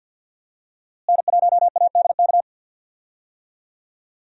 Morse code, heard as a keyed tone, about 52 wpm, 700 Hz. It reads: N1ADK
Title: USA Call Signs - 5 Characters - Repeated 3x - 50wpm